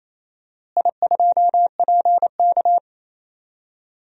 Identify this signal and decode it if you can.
Morse code, heard as a keyed tone, about 28 wpm, 700 Hz. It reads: I2PK
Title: DXCC-100 Call Signs - 4 Characters - Repeated 3x - 28wpm